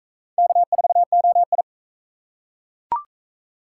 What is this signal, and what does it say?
Morse code, heard as a keyed tone, about 41 wpm, 700 Hz. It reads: K4OI